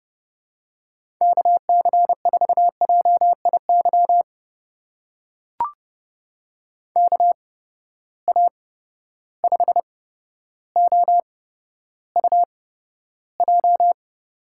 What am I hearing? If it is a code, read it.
Morse code, heard as a keyed tone, about 30 wpm, spaced as for 13 wpm, 700 Hz. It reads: KC4JIY KA5OUJ
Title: USA Call Signs - 6 Characters - 8x Character Spacing - 30wpm